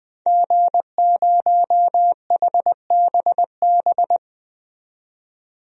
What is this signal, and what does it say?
Morse code, heard as a keyed tone, about 20 wpm, 700 Hz. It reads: G0HBB
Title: DXCC-100 Call Signs - 5 Characters - 20wpm